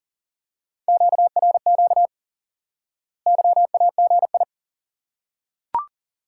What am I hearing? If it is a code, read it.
Morse code, heard as a keyed tone, about 40 wpm, 700 Hz. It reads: QRQ YAGI
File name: IQE Course 3 - Lesson 89 - YAGI - Two Elements - 40wpm